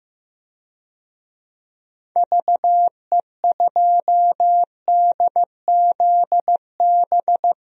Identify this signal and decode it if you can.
Morse code, heard as a keyed tone, about 15 wpm, 700 Hz. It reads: VE2DZB
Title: DXCC-100 Call Signs - 6 Characters - 15wpm